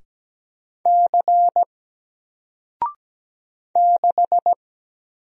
Morse code, heard as a keyed tone, about 17 wpm, 700 Hz. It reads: C 6
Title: Single Letter-Number Alt 17wpm